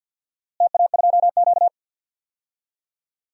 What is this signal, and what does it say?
Morse code, heard as a keyed tone, about 51 wpm, 700 Hz. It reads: TA2Q
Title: DXCC-100 Call Signs - 4 Characters - Repeated 2x - 50wpm